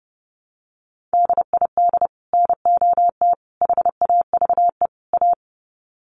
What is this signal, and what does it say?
Morse code, heard as a keyed tone, about 30 wpm, 700 Hz. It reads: DID NOT HAVE A
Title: Sets of 4 Words 30wpm